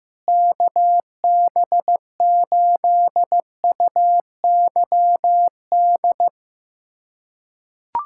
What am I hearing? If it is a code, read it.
Morse code, heard as a keyed tone, about 16 wpm, 700 Hz. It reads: KB8UYD
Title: USA Callsign DE Callsign 15wpm